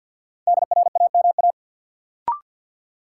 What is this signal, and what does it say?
Morse code, heard as a keyed tone, about 53 wpm, 700 Hz. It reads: DRAMA
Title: Sentences from Top 1500 Words - New Word Drill - 50wpm